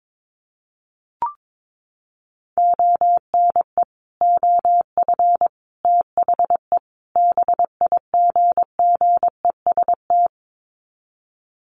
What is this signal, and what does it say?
Morse code, heard as a keyed tone, about 22 wpm, 700 Hz. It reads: ONE OF THE BIGGEST
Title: Sets of 4 Words 22wpm